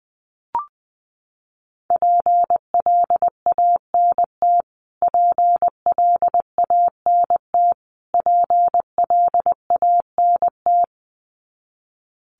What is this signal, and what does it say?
Morse code, heard as a keyed tone, about 20 wpm, 700 Hz. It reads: PLANT PLANT PLANT
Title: Top 500 Words Repeated 20wpm